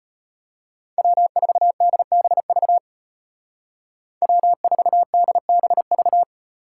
Morse code, heard as a keyed tone, about 31 wpm, 700 Hz. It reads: W4DBV W4DBV
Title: Speed Racing - USA 1x3 Call Signs 25wpm